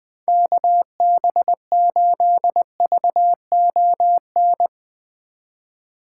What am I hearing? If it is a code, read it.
Morse code, heard as a keyed tone, about 20 wpm, 700 Hz. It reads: KB8VON
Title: 2023 POTA Hunter Call Signs - 20wpm